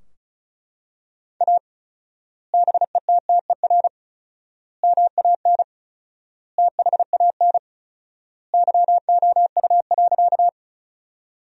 Morse code, heard as a keyed tone, about 34 wpm, 700 Hz. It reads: A BETTER MAN THAN YOU.
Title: 4x Word-Spacing - Sentences from Top 400 Words - 35wpm